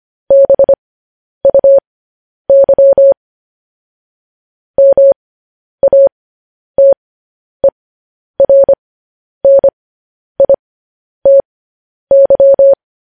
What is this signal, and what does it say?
Morse code, heard as a keyed tone, about 25 wpm, spaced as for 10 wpm, 550 Hz. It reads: BUY MATERNITY